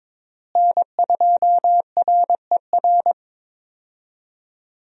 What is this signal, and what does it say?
Morse code, heard as a keyed tone, about 22 wpm, 700 Hz. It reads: N2RER
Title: USA Call Signs - 5 Characters - Repeated 2x - 22wpm